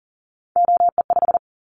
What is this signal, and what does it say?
Morse code, heard as a keyed tone, about 40 wpm, 700 Hz. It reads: OE5
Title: Callsign Prefixes for Top 100 DXCC - 40wpm